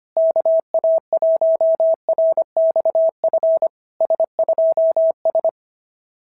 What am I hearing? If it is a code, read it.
Morse code, heard as a keyed tone, about 25 wpm, 660 Hz. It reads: KA1RXF S2S
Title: SOTA QSO Practice 25wpm